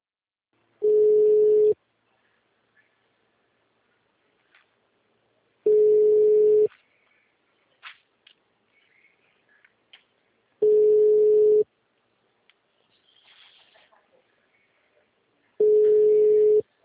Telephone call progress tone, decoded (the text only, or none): ringback tone